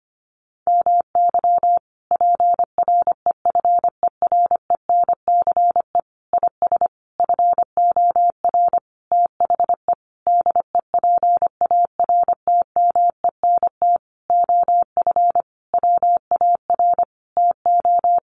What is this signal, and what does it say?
Morse code, heard as a keyed tone, about 25 wpm, 700 Hz. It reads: MY PREFERENCE IS FOR THE DEPARTMENT OF WAR TO
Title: Sentences from Top 1500 Words 25wpm